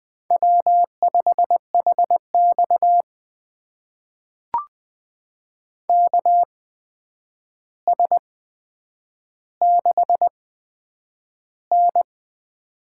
Morse code, heard as a keyed tone, about 20 wpm, spaced as for 9 wpm, 700 Hz. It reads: W5HX KS6N
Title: USA Call Signs - 8x Character Spacing - 20wpm